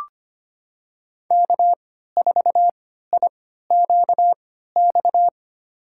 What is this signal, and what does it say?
Morse code, heard as a keyed tone, about 25 wpm, spaced as for 14 wpm, 700 Hz. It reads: K4IQX
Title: USA Call Signs - 5 Characters - 3x Character Spacing - 25wpm